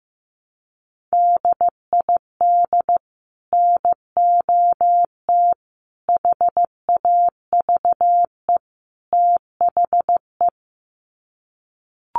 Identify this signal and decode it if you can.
Morse code, heard as a keyed tone, about 15 wpm, 700 Hz. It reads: DID NOT HAVE THE
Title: Sets of 4 Words 15wpm